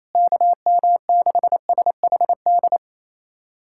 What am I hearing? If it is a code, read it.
Morse code, heard as a keyed tone, about 28 wpm, 700 Hz. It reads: KM6SHD
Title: USA Callsign DE Callsign 28wpm